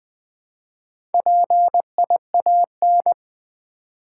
Morse code, heard as a keyed tone, about 20 wpm, 700 Hz. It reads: PIAN
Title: Build A Word - 1 Letter At A Time - 5-Letter Words - 20wpm